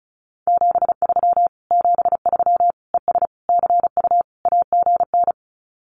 Morse code, heard as a keyed tone, about 36 wpm, 700 Hz. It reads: 73 73 ES CU AGN
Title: Common QSO Phrases Encore 35wpm